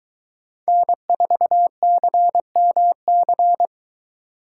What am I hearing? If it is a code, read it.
Morse code, heard as a keyed tone, about 23 wpm, 700 Hz. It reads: N4CMC